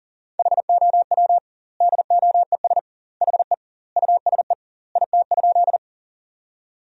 Morse code, heard as a keyed tone, about 40 wpm, 700 Hz. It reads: HOW DOES HE USE IT?
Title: 2x Word-Spacing - Sentences from Top 200 Words - 40wpm